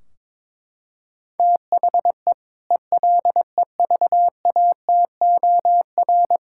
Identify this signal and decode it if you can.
Morse code, heard as a keyed tone, about 22 wpm, 700 Hz. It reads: THE ELEVATOR
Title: Sentences from Top 1600 Words 22wpm